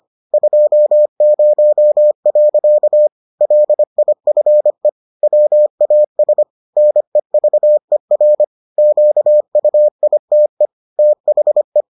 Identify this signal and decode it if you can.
Morse code, heard as a keyed tone, about 25 wpm, 600 Hz. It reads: 20. LIFE WAS NEVER QUITE THE